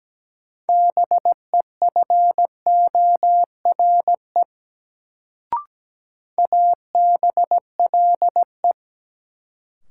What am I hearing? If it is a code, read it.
Morse code, heard as a keyed tone, about 17 wpm, 700 Hz. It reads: BEFORE ABLE
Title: Sets of 2 Words - New Word Drill - 17wpm